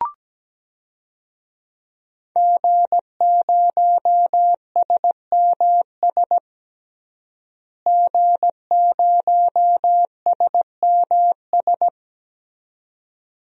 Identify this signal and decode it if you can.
Morse code, heard as a keyed tone, about 17 wpm, 700 Hz. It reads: G0SMS G0SMS
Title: DXCC-100 Call Signs - 5 Characters - Repeated 2x - 17wpm